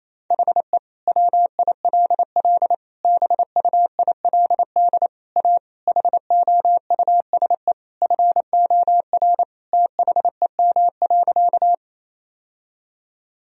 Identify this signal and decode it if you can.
Morse code, heard as a keyed tone, about 27 wpm, 700 Hz. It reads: HE WILL BUILD A HOUSE FOR THEM.
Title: Sentences from Top 1000 Words - Review Part 1 - 28wpm